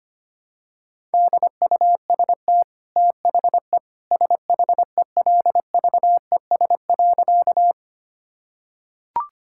Morse code, heard as a keyed tone, about 25 wpm, 690 Hz. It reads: DUST THE SHELVES.
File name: Easy Sentences with Hints - 25wpm